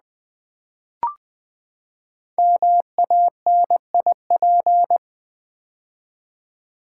Morse code, heard as a keyed tone, about 20 wpm, 700 Hz. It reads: MANIP
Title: Build A Word - 5 Letters At A Time - 12-Letter Words - 20wpm